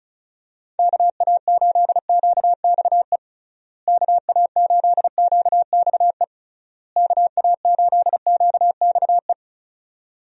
Morse code, heard as a keyed tone, about 36 wpm, 700 Hz. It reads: KA8QXE KA8QXE KA8QXE
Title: USA Call Signs - 6 Characters - Repeated 3x - 35wpm